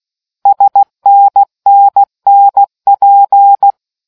Morse code, heard as a keyed tone, about 16 wpm, 800 Hz. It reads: SNNNP